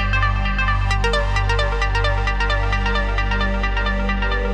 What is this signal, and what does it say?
There is no code.